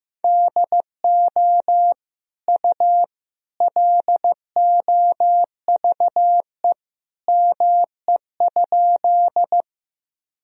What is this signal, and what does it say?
Morse code, heard as a keyed tone, about 15 wpm, 700 Hz. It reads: DO U LOVE ME?
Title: Sentences from Top 400 Words and Abbreviations 15wpm